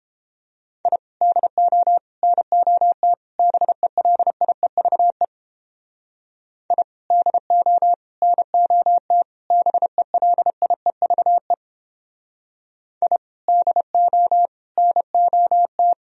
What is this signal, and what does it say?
Morse code, heard as a keyed tone, about 31 wpm, 700 Hz. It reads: I DO NOT BELIEVE I DO NOT BELIEVE I DO NOT